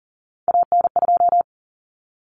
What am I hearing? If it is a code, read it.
Morse code, heard as a keyed tone, about 40 wpm, 700 Hz. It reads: AN2
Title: Common Cut Numbers - 3-Digit 40wpm